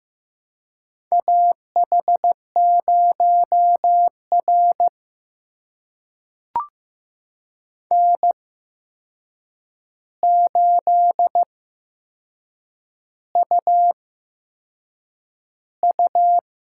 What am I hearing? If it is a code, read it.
Morse code, heard as a keyed tone, about 15 wpm, spaced as for 6 wpm, 700 Hz. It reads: AH0R N8UU
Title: USA Call Signs - 4 Characters - 8x Character Spacing - 15wpm